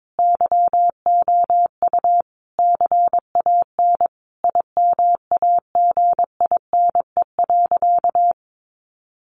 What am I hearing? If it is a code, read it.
Morse code, heard as a keyed tone, about 22 wpm, 700 Hz. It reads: YOU CAN IMAGINE.
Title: Sentences from Top 1500 Words 22wpm